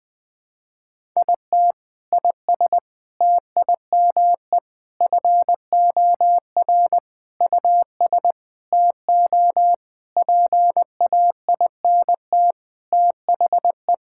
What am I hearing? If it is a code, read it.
Morse code, heard as a keyed tone, about 20 wpm, 700 Hz. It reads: IT IS TIME FOR US TO PAINT THE
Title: Sentences from Top 1000 Words - Review Part 6 - 20wpm